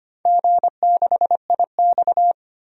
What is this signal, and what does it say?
Morse code, heard as a keyed tone, about 25 wpm, 700 Hz. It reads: G6IX
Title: DXCC-100 Call Signs - 4 Characters - 25wpm